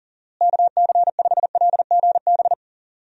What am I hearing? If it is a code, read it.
Morse code, heard as a keyed tone, about 37 wpm, 700 Hz. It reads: KC5LGB
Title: USA Call Signs - 6 Characters - Repeated 3x - 40wpm